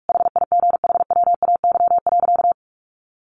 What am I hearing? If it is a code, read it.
Morse code, heard as a keyed tone, about 44 wpm, 700 Hz. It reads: HIGHWAY.
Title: Sentences from Top 1400 Words 45wpm